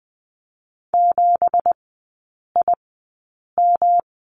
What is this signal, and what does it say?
Morse code, heard as a keyed tone, about 20 wpm, spaced as for 8 wpm, 700 Hz. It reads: 7IM
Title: 178 Lesson 29e - Slash - Three Characters 20wpm